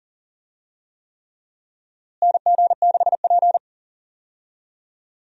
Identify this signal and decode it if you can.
Morse code, heard as a keyed tone, about 40 wpm, 700 Hz. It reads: NG6P
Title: USA Call Signs - 4 Characters - Repeated 3x - 40wpm